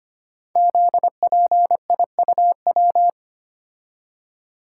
Morse code, heard as a keyed tone, about 25 wpm, 700 Hz. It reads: ZPIUW
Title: Code Groups - 5 Letters - 25wpm